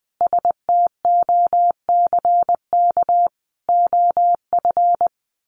Morse code, heard as a keyed tone, about 20 wpm, 700 Hz. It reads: STOCK OF
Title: Sentences from Top 1500 Words 20wpm